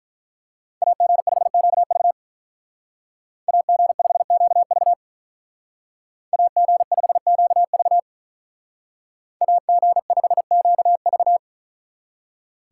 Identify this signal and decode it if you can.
Morse code, heard as a keyed tone, about 43 wpm, 700 Hz. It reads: AG5QV AG5QV AG5QV AG5QV
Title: Speed Racing - USA Call Signs 35wpm